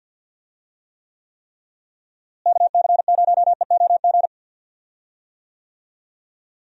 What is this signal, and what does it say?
Morse code, heard as a keyed tone, about 49 wpm, 700 Hz. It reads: KC0EOG
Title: USA Call Signs - 6 Characters - Repeated 3x - 50wpm